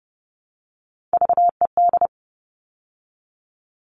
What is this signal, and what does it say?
Morse code, heard as a keyed tone, about 30 wpm, 700 Hz. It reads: VED
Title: All Cut Numbers - 3-Digit 30wpm